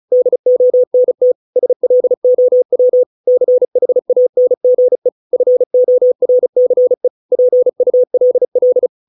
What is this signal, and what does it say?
Morse code, heard as a keyed tone, about 34 wpm, 500 Hz. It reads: DONT SLOW CHANGE FORCE PULL